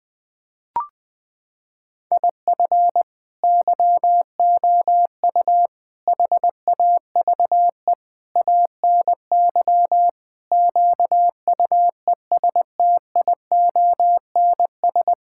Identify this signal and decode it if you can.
Morse code, heard as a keyed tone, about 20 wpm, 700 Hz. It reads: IF YOU HAVE ANY QUESTIONS
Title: Sentences with 8-Word Phrases - 20wpm